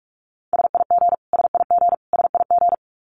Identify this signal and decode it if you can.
Morse code, heard as a keyed tone, about 44 wpm, 700 Hz. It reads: SIG SIG SIG
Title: Abbreviations Repeated 3x - 45wpm